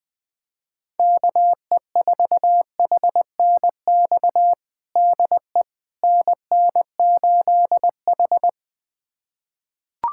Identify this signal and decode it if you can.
Morse code, heard as a keyed tone, about 20 wpm, 700 Hz. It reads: KE4HNX DE NN8H
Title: USA Callsign DE Callsign 20wpm